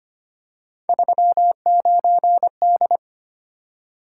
Morse code, heard as a keyed tone, about 25 wpm, 700 Hz. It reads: 39D
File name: Three Letter-Number Combinations - 25wpm